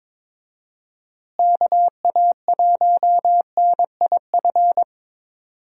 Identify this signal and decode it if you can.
Morse code, heard as a keyed tone, about 22 wpm, 700 Hz. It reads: KA1NIF
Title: USA Call Signs - 6 Characters - Repeated 3x - 22wpm